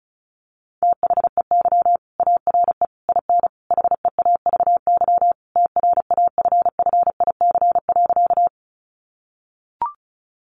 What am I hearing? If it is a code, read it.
Morse code, heard as a keyed tone, about 34 wpm, 700 Hz. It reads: THEY ARE IN HEAVY TRAFFIC.
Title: Sentences from Top 1300 Words 35wpm